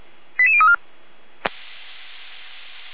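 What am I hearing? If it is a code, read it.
Five-tone selective call (ZVEI), heard as quick runs of five tones, tones of about 70 ms. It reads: 88034
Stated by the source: Five-tone ZVEI selective call received on an SDR.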